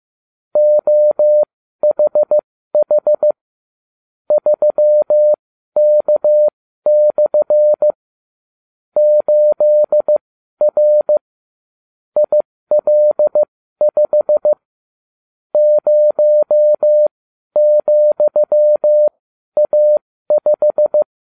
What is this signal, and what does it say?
Morse code, heard as a keyed tone, about 15 wpm, spaced as for 12 wpm, 600 Hz. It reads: OHH 3K/ 8R IL5 0,A5